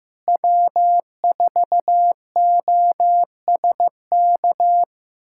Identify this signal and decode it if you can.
Morse code, heard as a keyed tone, about 15 wpm, 700 Hz. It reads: W4OSK
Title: USA Callsign DE Callsign 15wpm